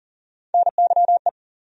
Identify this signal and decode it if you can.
Morse code, heard as a keyed tone, about 40 wpm, 700 Hz. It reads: NYE
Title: USA Call Sign Postfixes - 3 Letters 40wpm